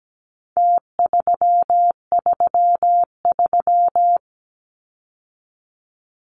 Morse code, heard as a keyed tone, about 17 wpm, 700 Hz. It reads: T333
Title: Numbers - Contest Style - 4-Digit 17wpm